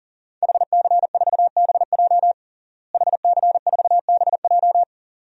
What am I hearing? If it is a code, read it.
Morse code, heard as a keyed tone, about 40 wpm, 700 Hz. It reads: HC4BJ HC4BJ
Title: DXCC-100 Call Signs - 5 Characters - Repeated 2x - 40wpm